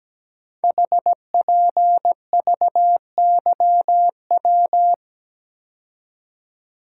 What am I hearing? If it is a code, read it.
Morse code, heard as a keyed tone, about 17 wpm, 700 Hz. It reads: HPVYW
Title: Code Groups - 5 Letters - 17wpm